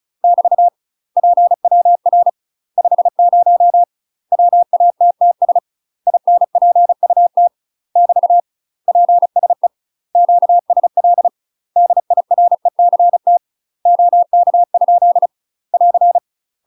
Morse code, heard as a keyed tone, about 36 wpm, 700 Hz. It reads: <BT> PWR 50 WATTS INPUT <BT> PSE QSL DIRECT OK? <AR>
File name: UK MASTER QSOs TRAINING / 35WPM / UK MASTER QSOs 0031